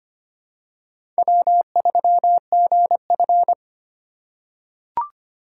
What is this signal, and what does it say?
Morse code, heard as a keyed tone, about 25 wpm, 700 Hz. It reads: W3GF
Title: USA Call Signs - 4 Characters - 2x Character Spacing - 25wpm